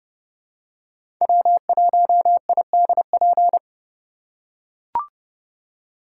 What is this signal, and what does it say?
Morse code, heard as a keyed tone, about 31 wpm, 700 Hz. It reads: W1IDP
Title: USA Call Signs - 5 Characters - 2x Character Spacing - 30wpm